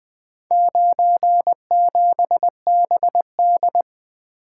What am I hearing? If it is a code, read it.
Morse code, heard as a keyed tone, about 24 wpm, 700 Hz. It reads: 97BD
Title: Speed-Racing - Four Letter-Number Combinations - 20wpm